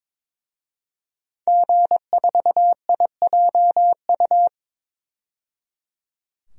Morse code, heard as a keyed tone, about 22 wpm, 700 Hz. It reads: G4IJU
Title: DXCC-100 Call Signs - 5 Characters - 22wpm